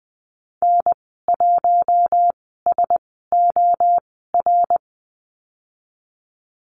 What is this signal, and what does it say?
Morse code, heard as a keyed tone, about 20 wpm, spaced as for 14 wpm, 700 Hz. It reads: N1SOR
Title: 122 Lesson 20g - Number 3 - Call Signs 20wpm